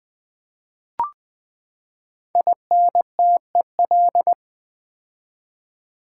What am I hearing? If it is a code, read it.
Morse code, heard as a keyed tone, about 20 wpm, 700 Hz. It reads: INTEL
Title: Build A Word - 5 Letters At A Time - 12-Letter Words - 20wpm